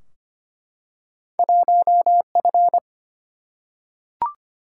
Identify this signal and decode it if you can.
Morse code, heard as a keyed tone, about 25 wpm, 700 Hz. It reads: 1 F
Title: Single-Letter Single-Number 25wpm